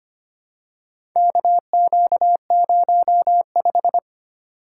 Morse code, heard as a keyed tone, about 25 wpm, 700 Hz. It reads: KQ05
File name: practice / 25wpm / Four Letter-Number Combinations - 25wpm